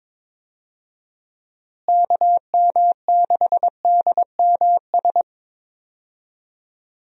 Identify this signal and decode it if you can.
Morse code, heard as a keyed tone, about 22 wpm, 700 Hz. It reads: KM6DMS
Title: USA Call Signs - 6 Characters - Repeated 3x - 22wpm